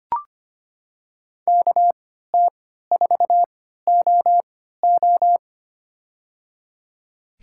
Morse code, heard as a keyed tone, about 25 wpm, spaced as for 14 wpm, 700 Hz. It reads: KT4OO
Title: USA Call Signs - 5 Characters - 3x Character Spacing - 25wpm